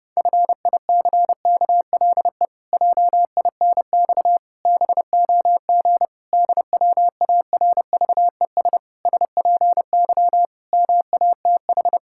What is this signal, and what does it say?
Morse code, heard as a keyed tone, about 30 wpm, 700 Hz. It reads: FICKLE JINX BOG DWARVES SPY MATH
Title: Pangram Challenge v2 - 30wpm